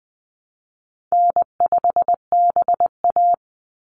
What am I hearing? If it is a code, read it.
Morse code, heard as a keyed tone, about 20 wpm, 700 Hz. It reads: N5BA
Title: Easy Call Signs - 20wpm